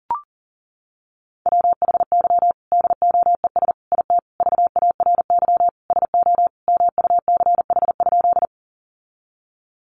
Morse code, heard as a keyed tone, about 40 wpm, 700 Hz. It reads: WHY DOES IT VARY SO MUCH?
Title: Sentences from Top 1200 Words 40wpm